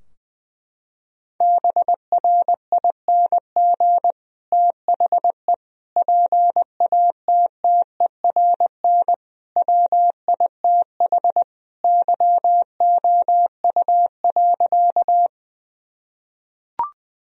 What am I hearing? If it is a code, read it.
Morse code, heard as a keyed tone, about 20 wpm, 700 Hz. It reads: BRING THE PATTERN WITH YOU.
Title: Sentences from Top 1000 Words - Review Part 2 - 20wpm